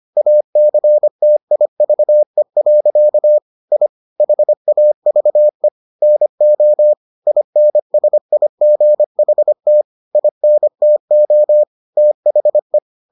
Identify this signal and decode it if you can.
Morse code, heard as a keyed tone, about 25 wpm, 600 Hz. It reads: ACTIVE. I HAVE NO INSIGHT INTO THE